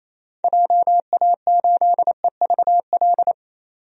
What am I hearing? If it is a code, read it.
Morse code, heard as a keyed tone, about 28 wpm, 700 Hz. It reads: JA8EVL
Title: DXCC-100 Call Signs - 6 Characters - 28wpm